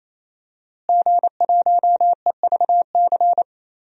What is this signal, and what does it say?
Morse code, heard as a keyed tone, about 27 wpm, 700 Hz. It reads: G1EVC
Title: DXCC-100 Call Signs - 5 Characters - Repeated 3x - 28wpm